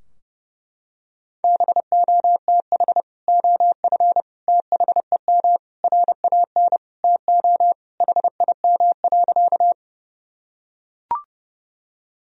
Morse code, heard as a keyed tone, about 30 wpm, 700 Hz. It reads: BOTH OF THEM RAN TO HIM.
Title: Sentences from Top 1000 Words - Review Part 3 - 30wpm